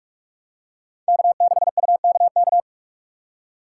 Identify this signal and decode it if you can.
Morse code, heard as a keyed tone, about 46 wpm, 700 Hz. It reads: K6UKK
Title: USA Call Signs - 2x Character Spacing - 45wpm